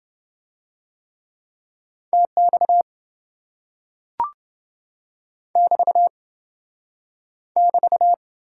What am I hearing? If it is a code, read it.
Morse code, heard as a keyed tone, about 27 wpm, 700 Hz. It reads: TX <BT> <BT>